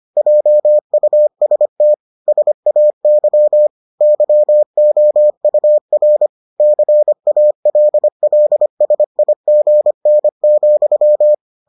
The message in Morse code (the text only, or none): JUST SAY YOUR CALLSIGN,